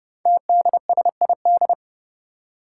Morse code, heard as a keyed tone, about 29 wpm, 700 Hz. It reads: TDSID
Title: Five Challenging Letter-Number Combinations 30wpm